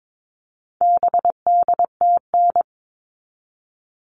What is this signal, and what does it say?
Morse code, heard as a keyed tone, about 22 wpm, 700 Hz. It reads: BDTN
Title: All Cut Numbers - 4-Digit 22wpm